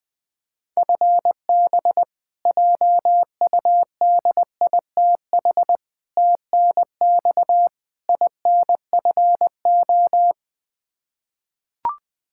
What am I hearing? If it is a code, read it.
Morse code, heard as a keyed tone, about 21 wpm, 700 Hz. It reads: FB JUDITH TNX INFO
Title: K1USN SST QSO Practice 20wpm